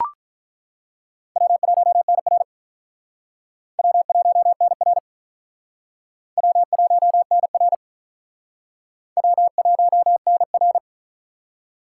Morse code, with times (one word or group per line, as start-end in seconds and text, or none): 1.4-2.4 W1NR
3.8-5.0 W1NR
6.4-7.7 W1NR
9.2-10.8 W1NR